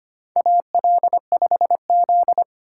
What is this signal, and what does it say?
Morse code, heard as a keyed tone, about 25 wpm, 700 Hz. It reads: AL5Z
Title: USA Call Signs - 4 Characters - 3x Character Spacing - 25wpm